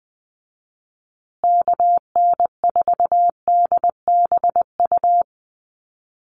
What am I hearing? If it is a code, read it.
Morse code, heard as a keyed tone, about 20 wpm, 700 Hz. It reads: KN4DBU
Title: USA Call Signs Repeated 3x 20wpm